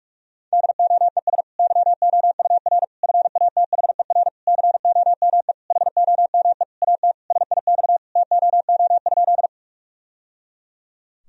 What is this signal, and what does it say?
Morse code, heard as a keyed tone, about 46 wpm, 700 Hz. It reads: DOES YOUR FATHER COME HOME AT SIX TOO?
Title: Sentences from Top 1000 Words - Review Part 2 - 45wpm